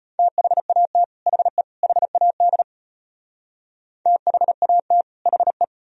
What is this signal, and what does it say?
Morse code, heard as a keyed tone, about 38 wpm, 700 Hz. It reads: THAT HE HAD THAT HE